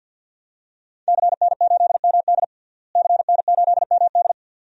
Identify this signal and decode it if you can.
Morse code, heard as a keyed tone, about 52 wpm, 700 Hz. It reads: CN8MD CN8MD
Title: DXCC-100 Call Signs - 5 Characters - Repeated 2x - 50wpm